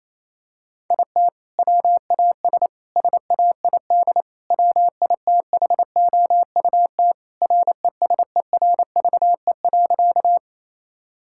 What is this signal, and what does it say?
Morse code, heard as a keyed tone, about 28 wpm, 700 Hz. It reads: IT WAS SAID WITHOUT RESERVE.
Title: Sentences from Top 1000 Words - Review Part 6 - 28wpm